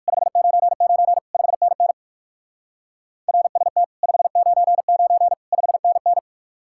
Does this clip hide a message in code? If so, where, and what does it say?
Yes: Morse code from 0.1 to 6.2 s, heard as a keyed tone, about 59 wpm, 700 Hz. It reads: 599 5NN RST 599 5NN